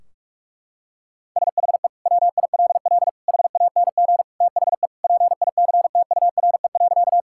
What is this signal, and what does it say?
Morse code, heard as a keyed tone, about 46 wpm, 700 Hz. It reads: SHE WILL HANG THE PICTURE.
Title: Sentences from Top 1600 Words 45wpm